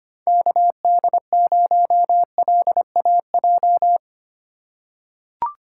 Morse code, heard as a keyed tone, about 25 wpm, 700 Hz. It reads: KD0LAJ
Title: USA Call Signs - 6 Characters - 2x Character Spacing - 25wpm